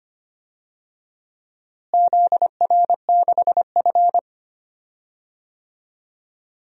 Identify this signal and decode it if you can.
Morse code, heard as a keyed tone, about 25 wpm, 700 Hz. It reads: ZR6F
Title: DXCC-100 Call Signs - 4 Characters - Repeated 2x - 25wpm